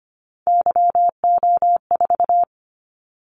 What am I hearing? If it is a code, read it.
Morse code, heard as a keyed tone, about 25 wpm, 700 Hz. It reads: YO4
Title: Callsign Prefixes for Top 25 DXCC - 25wpm